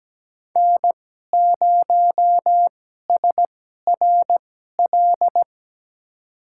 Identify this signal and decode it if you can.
Morse code, heard as a keyed tone, about 17 wpm, spaced as for 12 wpm, 700 Hz. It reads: N0SRL
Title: USA Call Signs - 5 Characters - 2x Character Spacing - 17wpm